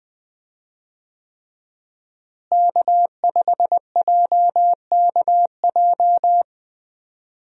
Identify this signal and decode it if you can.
Morse code, heard as a keyed tone, about 20 wpm, 700 Hz. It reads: K5JKJ
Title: USA Call Signs - 5 Characters - Repeated 3x - 20wpm